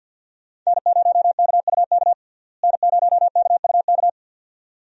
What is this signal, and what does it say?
Morse code, heard as a keyed tone, about 52 wpm, 700 Hz. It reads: N0KUK N0KUK
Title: USA Call Signs - 5 Characters - Repeated 2x - 50wpm